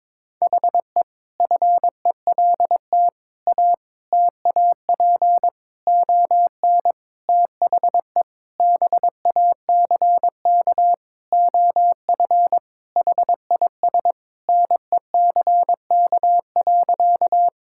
Morse code, heard as a keyed tone, about 22 wpm, 700 Hz. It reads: HE FELT A TAP ON THE BACK OF HIS NECK.
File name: practice / 22wpm / Sentences with 4-Word Phrases - 22wpm